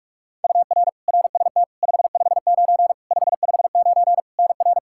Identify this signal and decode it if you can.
Morse code, heard as a keyed tone, about 46 wpm, 700 Hz. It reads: UR RST 559 559 NR
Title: Practice QSO - Exchange 1 - Beginning - 45wpm